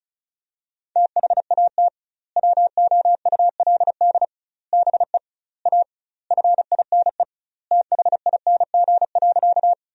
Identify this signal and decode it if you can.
Morse code, heard as a keyed tone, about 34 wpm, 700 Hz. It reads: THAT WOULD BE A FINE THING.
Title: 2x Word-Spacing - Sentences from Top 500 Words - 35wpm